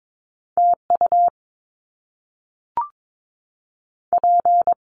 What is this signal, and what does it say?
Morse code, heard as a keyed tone, about 22 wpm, 700 Hz. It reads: T U P
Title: Top Two Letter Combinations in Words 22wpm